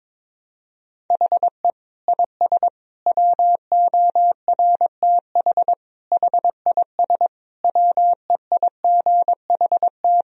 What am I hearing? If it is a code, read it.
Morse code, heard as a keyed tone, about 22 wpm, 700 Hz. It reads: HE IS WORTH HIS WEIGHT
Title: Sentences from Top 1000 Words - Review Part 3 - 22wpm